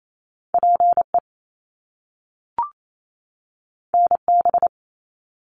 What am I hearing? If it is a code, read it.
Morse code, heard as a keyed tone, about 28 wpm, 700 Hz. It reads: PE NB
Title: Canadian Provinces and Territories 28wpm